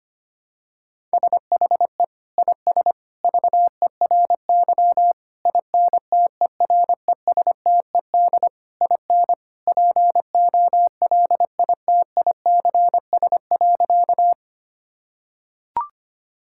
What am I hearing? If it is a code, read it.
Morse code, heard as a keyed tone, about 25 wpm, 700 Hz. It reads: SHE IS VERY INTERESTED IN POLITICS.
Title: Sentences from Top 1000 Words - Review Part 6 - 25wpm